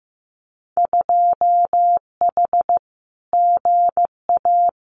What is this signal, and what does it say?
Morse code, heard as a keyed tone, about 15 wpm, 700 Hz. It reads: 2H GA
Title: Winter Field Day Exchanges Encore 15wpm